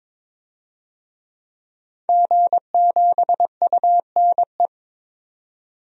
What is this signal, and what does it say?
Morse code, heard as a keyed tone, about 22 wpm, 700 Hz. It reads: G7UNE
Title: DXCC-100 Call Signs - 5 Characters - 22wpm